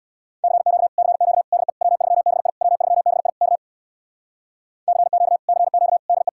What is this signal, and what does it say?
Morse code, heard as a keyed tone, about 66 wpm, 700 Hz. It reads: CQ CQ DE W3BI W3BI K CQ CQ DE